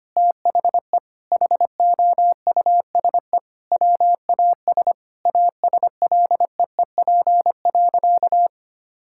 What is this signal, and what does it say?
Morse code, heard as a keyed tone, about 25 wpm, 700 Hz. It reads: THE HOUSE WAS ASLEEP.
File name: practice / 25wpm / Sentences with 3-Word Phrases - 25wpm